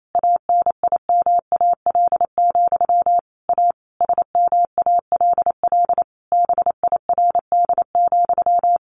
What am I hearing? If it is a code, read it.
Morse code, heard as a keyed tone, about 28 wpm, 700 Hz. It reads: ANIMAL, A SMALL BIRD,
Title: A Sound Of Thunder 28wpm